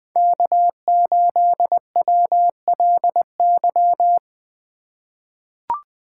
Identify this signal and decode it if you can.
Morse code, heard as a keyed tone, about 21 wpm, 700 Hz. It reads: K8WLY
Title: USA Call Signs - 5 Characters - 2x Character Spacing - 20wpm